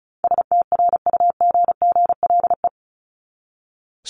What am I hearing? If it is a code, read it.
Morse code, heard as a keyed tone, about 34 wpm, 700 Hz. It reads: STRUGGLE
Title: Top 1751-2000 Words 35wpm